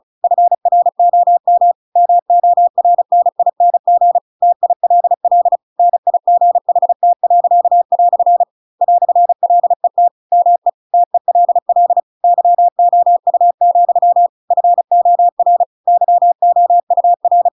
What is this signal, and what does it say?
Morse code, heard as a keyed tone, about 37 wpm, 700 Hz. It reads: FROM MORNING TILL NIGHT." "LET ME TELL YOU, FOR YOUR